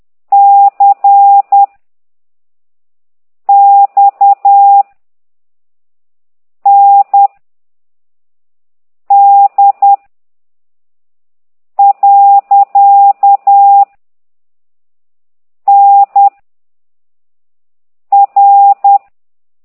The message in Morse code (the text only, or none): C X N D . N R